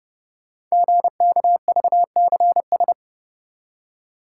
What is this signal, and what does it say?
Morse code, heard as a keyed tone, about 28 wpm, 700 Hz. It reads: GKVCS
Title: Code Groups - 5 Letters - 30wpm